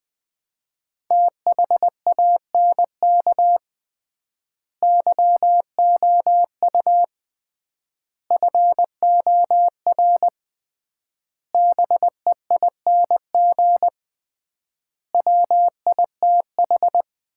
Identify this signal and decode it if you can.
Morse code, heard as a keyed tone, about 20 wpm, 700 Hz. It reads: THANK YOU FOR BEING WITH
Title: 3x Word-Spacing - Sets of 5 Words - 20wpm